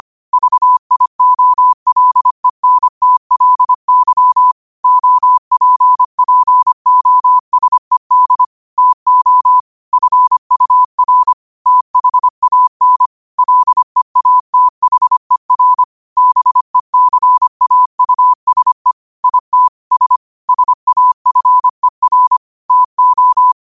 Morse code, heard as a keyed tone, about 25 wpm, 1000 Hz. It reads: VIOLENTLY OPPOSED TO FUR THAN LEATHER BECAUSE ITS SAFER TO
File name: 25wpm